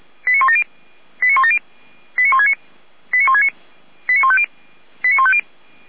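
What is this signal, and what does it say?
Five-tone selective call (ZVEI), heard as quick runs of five tones, tones of about 70 ms. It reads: 89170 89170 89169 89169 89150 89150
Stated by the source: ZVEI five-tone selective call 89220 received with an SDR, fire brigade Feuerwehr Orlamünde.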